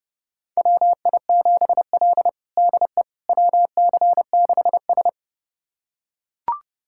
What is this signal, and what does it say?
Morse code, heard as a keyed tone, about 30 wpm, 700 Hz. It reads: WI7L DE WC6S
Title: USA Callsign DE Callsign - 4 Characters - 30wpm